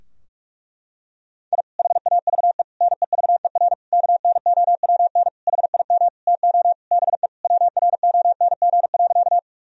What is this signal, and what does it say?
Morse code, heard as a keyed tone, about 45 wpm, 700 Hz. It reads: I HAVE NEVER KNOWN HIM TO BE WRONG.
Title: Sentences from Top 1000 Words - Review Part 3 - 45wpm